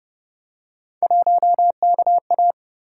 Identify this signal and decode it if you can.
Morse code, heard as a keyed tone, about 32 wpm, 700 Hz. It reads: 1KA